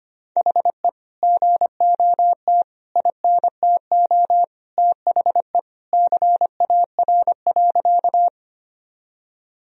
Morse code, heard as a keyed tone, about 25 wpm, 700 Hz. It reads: HE GOT INTO THE CAR.
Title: Sentences from Top 1000 Words - Review Part 1 - 25wpm